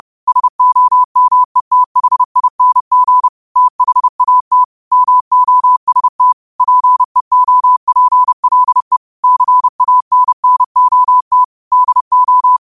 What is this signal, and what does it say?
Morse code, heard as a keyed tone, about 30 wpm, 1000 Hz. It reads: SOMETHING THAT MOST PEOPLE CANNOT DO
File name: qotdfiles / 30wpm